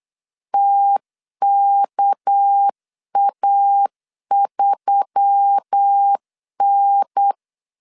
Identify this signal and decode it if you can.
Morse code, heard as a keyed tone, about 8 wpm, 800 Hz. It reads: TKA3N
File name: LEZIONI / 7 GF9Z / 7 G 8wpm 3118